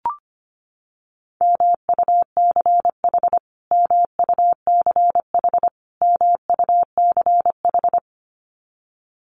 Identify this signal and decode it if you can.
Morse code, heard as a keyed tone, about 25 wpm, 700 Hz. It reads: MUCH MUCH MUCH
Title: Top 500 Words Repeated 25wpm